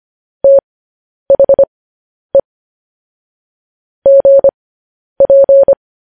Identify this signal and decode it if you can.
Morse code, heard as a keyed tone, about 25 wpm, spaced as for 10 wpm, 550 Hz. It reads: THE GP